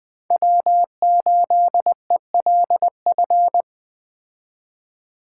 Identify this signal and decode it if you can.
Morse code, heard as a keyed tone, about 20 wpm, 700 Hz. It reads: W8ELF
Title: Speed Racing - USA 1x3 Call Signs 17wpm